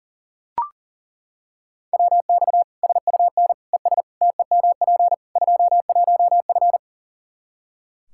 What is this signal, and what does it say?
Morse code, heard as a keyed tone, about 40 wpm, 700 Hz. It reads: WX SUN ES TEMP 21F
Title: Practice QSO - Exchange 2 - Info - 40wpm